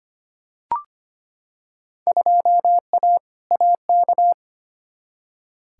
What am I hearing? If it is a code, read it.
Morse code, heard as a keyed tone, about 25 wpm, 700 Hz. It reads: 2A AK
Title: Field Day Exchanges Encore - 25wpm